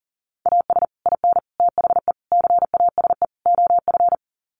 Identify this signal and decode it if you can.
Morse code, heard as a keyed tone, about 40 wpm, 700 Hz. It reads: AS IN THE CASE OF
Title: Sets of 5 Words 40wpm